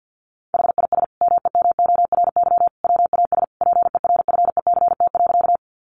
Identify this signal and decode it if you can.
Morse code, heard as a keyed tone, about 50 wpm, 700 Hz. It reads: HIS MEMORY WAS PERFECT.
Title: Sentences from Top 1300 Words 50wpm